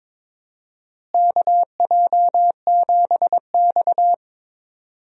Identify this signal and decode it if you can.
Morse code, heard as a keyed tone, about 22 wpm, 700 Hz. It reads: KJ7X
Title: USA Call Signs - 4 Characters - Repeated 3x - 22wpm